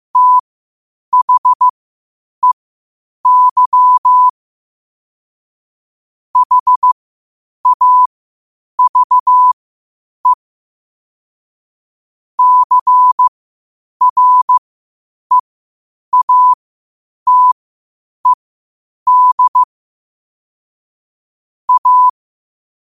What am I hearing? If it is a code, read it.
Morse code, heard as a keyed tone, about 15 wpm, spaced as for 8 wpm, 1000 Hz. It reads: THEY HAVE CREATED A
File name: qotdfiles / 8wpm